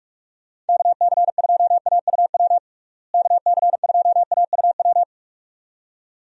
Morse code, heard as a keyed tone, about 46 wpm, 700 Hz. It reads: KC2AUW KC2AUW
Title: USA Call Signs - 6 Characters - Repeated 2x - 45wpm